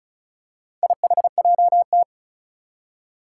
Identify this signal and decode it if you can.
Morse code, heard as a keyed tone, about 36 wpm, 700 Hz. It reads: IHJT
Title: Four Challenging Letter-Number Combinations 35wpm